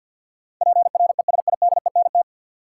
Morse code, heard as a keyed tone, about 49 wpm, 700 Hz. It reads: PRESIDENT
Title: Sets of 2 Words - New Word Drill - 50wpm